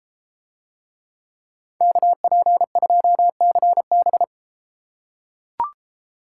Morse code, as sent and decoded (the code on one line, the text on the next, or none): -.- .--. ..--- -.-. -...
KP2CB